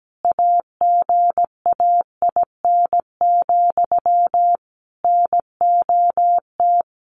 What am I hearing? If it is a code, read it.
Morse code, heard as a keyed tone, about 17 wpm, 700 Hz. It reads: AGAIN, NOT
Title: A Sound Of Thunder 17wpm